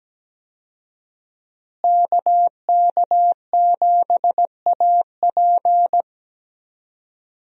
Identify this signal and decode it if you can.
Morse code, heard as a keyed tone, about 17 wpm, 700 Hz. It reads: KK7AP
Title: USA Call Signs - 5 Characters - Repeated 3x - 17wpm